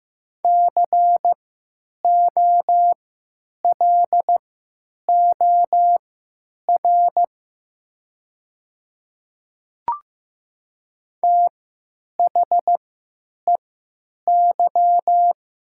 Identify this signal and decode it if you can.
Morse code, heard as a keyed tone, about 15 wpm, spaced as for 9 wpm, 700 Hz. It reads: COLOR THEY
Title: 3x Character Spacing - Top 300 Words - 15wpm